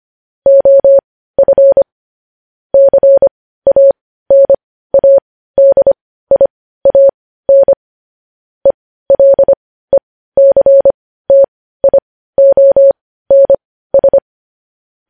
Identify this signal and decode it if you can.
Morse code, heard as a keyed tone, about 25 wpm, spaced as for 15 wpm, 550 Hz. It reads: OF CANADIAN ELECTIONS
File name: podcast / News headlines-1x-25x15